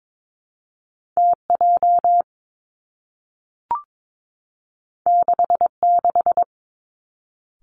Morse code, heard as a keyed tone, about 22 wpm, 700 Hz. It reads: T J 6 6
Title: Two Challenging Letter-Number Combinations 22wpm